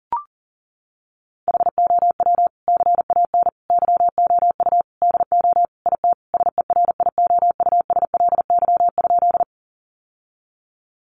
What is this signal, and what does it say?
Morse code, heard as a keyed tone, about 37 wpm, 700 Hz. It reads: HOW CAN YOU DO IT SERIOUSLY?
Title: Sentences from Top 1200 Words 40wpm